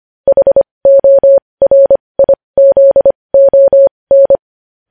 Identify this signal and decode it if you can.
Morse code, heard as a keyed tone, about 25 wpm, spaced as for 20 wpm, 550 Hz. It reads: HORIZON
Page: news headlines in morse code at 20 wpm